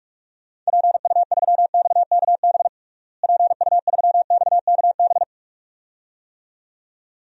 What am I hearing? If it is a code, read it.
Morse code, heard as a keyed tone, about 46 wpm, 700 Hz. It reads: PU3XKB PU3XKB
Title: DXCC-100 Call Signs - 6 Characters - Repeated 2x - 45wpm